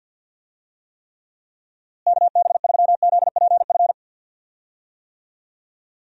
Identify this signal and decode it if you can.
Morse code, heard as a keyed tone, about 50 wpm, 700 Hz. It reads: KB3ZPF
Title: USA Call Signs - 6 Characters - Repeated 3x - 50wpm